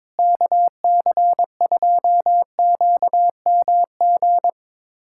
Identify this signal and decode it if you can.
Morse code, heard as a keyed tone, about 22 wpm, 700 Hz. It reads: KC2QMG
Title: USA Callsign DE Callsign - 6 Characters - 22wpm